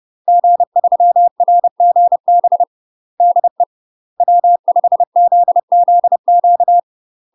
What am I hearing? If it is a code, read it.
Morse code, heard as a keyed tone, about 30 wpm, 700 Hz. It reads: G3RGB DE W5ZZQ
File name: UK MASTER QSOs TRAINING / 30WPM / UK MASTER QSOs 0020